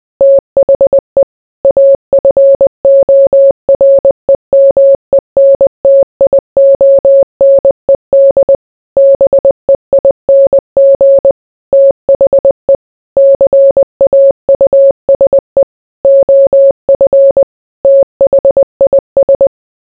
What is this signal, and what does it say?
Morse code, heard as a keyed tone, about 20 wpm, 550 Hz. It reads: THE AFOREMENTIONED BEING THE CAUSE OF THIS